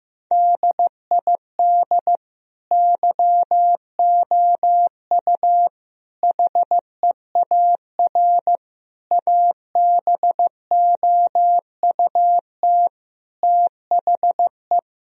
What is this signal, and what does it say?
Morse code, heard as a keyed tone, about 16 wpm, 700 Hz. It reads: DID YOU HEAR ABOUT THE
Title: Sentences from Top 1000 Words - Review Part 4 - 15wpm